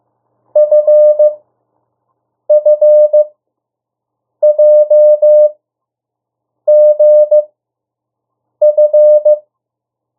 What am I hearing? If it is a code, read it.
Morse code, heard as a keyed tone, about 15 wpm, spaced as for 6 wpm, 600 Hz. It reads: FFJGF
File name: urok 1